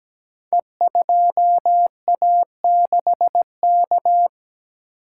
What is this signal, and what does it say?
Morse code, heard as a keyed tone, about 17 wpm, 700 Hz. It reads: E2A6K
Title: Code Groups - 5 Letter:Numbers - 17wpm